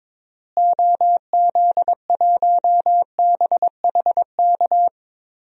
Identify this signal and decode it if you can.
Morse code, heard as a keyed tone, about 22 wpm, 700 Hz. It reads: OZ1BHK
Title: DXCC-100 Call Signs - 6 Characters - Repeated 3x - 22wpm